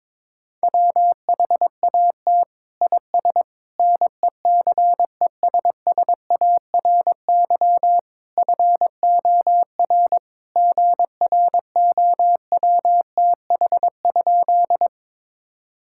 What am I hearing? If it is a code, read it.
Morse code, heard as a keyed tone, about 22 wpm, 700 Hz. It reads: WHAT IS NECESSARY FOR GROWTH?
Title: Sentences from Top 1000 Words - Review Part 5 - 22wpm